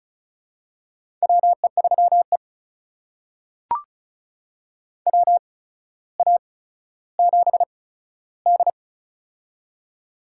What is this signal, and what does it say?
Morse code, heard as a keyed tone, about 36 wpm, spaced as for 15 wpm, 700 Hz. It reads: WE3E WA7D
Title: USA Call Signs - 4 Characters - 8x Character Spacing - 35wpm